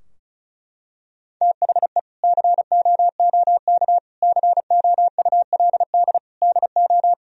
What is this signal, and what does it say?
Morse code, heard as a keyed tone, about 36 wpm, 700 Hz. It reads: THE COOK COULD DO
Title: Sentences from Top 1000 Words - Review Part 4 - 35wpm